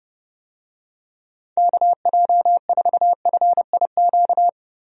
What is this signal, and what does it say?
Morse code, heard as a keyed tone, about 30 wpm, 700 Hz. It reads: KJ4FIQ
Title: USA Call Signs - 6 Characters - Repeated 3x - 30wpm